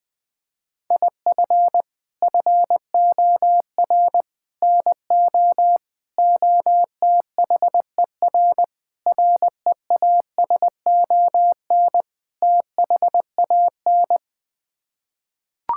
Morse code, heard as a keyed tone, about 20 wpm, 700 Hz. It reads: IF FOR NO OTHER REASON THAN
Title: Sets of 6 Words - 20wpm